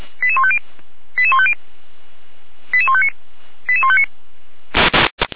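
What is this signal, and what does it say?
Five-tone selective call (ZVEI), heard as quick runs of five tones, tones of about 70 ms. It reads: 88150 88150 88159 88159
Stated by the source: ZVEI five-tone selective call received on SDR.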